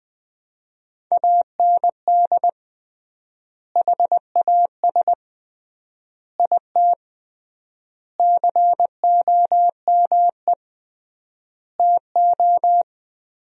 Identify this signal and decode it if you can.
Morse code, heard as a keyed tone, about 20 wpm, 700 Hz. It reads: AND HAS IT COME TO
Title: 3x Word-Spacing - Sentences from Top 100 Words - 20wpm